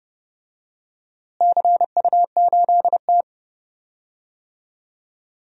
Morse code, heard as a keyed tone, about 30 wpm, 700 Hz. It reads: CU8T
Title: DXCC-100 Call Signs - 4 Characters - Repeated 2x - 30wpm